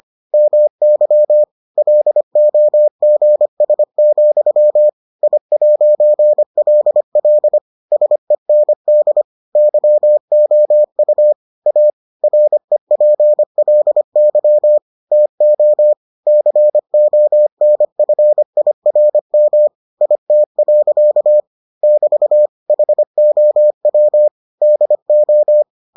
Morse code, heard as a keyed tone, about 25 wpm, 600 Hz. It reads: MY LOGS, I'LL SEND YOU A REPLY TO CONFIRM IT. <BT> HOW DO